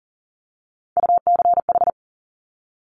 Morse code, heard as a keyed tone, about 40 wpm, 700 Hz. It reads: UCH
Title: Top Three Letter Combinations in Words 40wpm